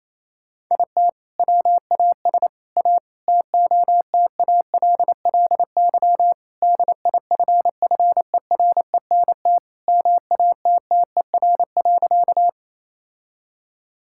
Morse code, heard as a keyed tone, about 28 wpm, 700 Hz. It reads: IT WAS A TOTALLY DIFFERENT MATTER.
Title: Sentences from Top 1600 Words 28wpm